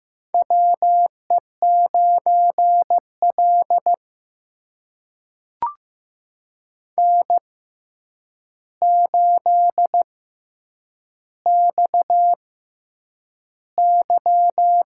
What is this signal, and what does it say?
Morse code, heard as a keyed tone, about 15 wpm, spaced as for 8 wpm, 700 Hz. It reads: WE9L N8XY
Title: USA Call Signs - 4 Characters - 6x Character Spacing - 15wpm